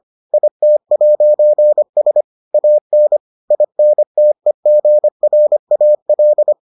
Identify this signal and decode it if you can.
Morse code, heard as a keyed tone, about 25 wpm, 600 Hz. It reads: IT'S AN INTEGRAL